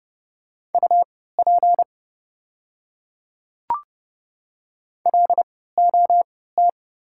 Morse code, heard as a keyed tone, about 30 wpm, spaced as for 17 wpm, 700 Hz. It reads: UP LOT
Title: 3x Character Spacing - Top 500 Words - 30wpm